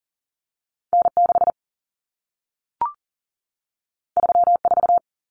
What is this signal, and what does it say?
Morse code, heard as a keyed tone, about 40 wpm, 700 Hz. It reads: N6 34
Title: Common Cut Numbers - 2-Digit 40wpm